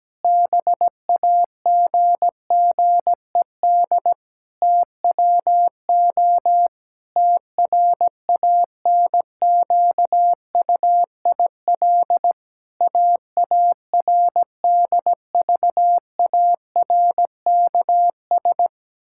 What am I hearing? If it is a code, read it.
Morse code, heard as a keyed tone, about 17 wpm, 700 Hz. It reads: BAGGED TWO TRANQUIL AARDVARKS
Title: Pangram Challenge v2 - 17wpm